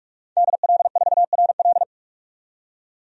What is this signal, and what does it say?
Morse code, heard as a keyed tone, about 47 wpm, 700 Hz. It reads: DL4RL
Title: DXCC-100 Call Signs - 5 Characters - 45wpm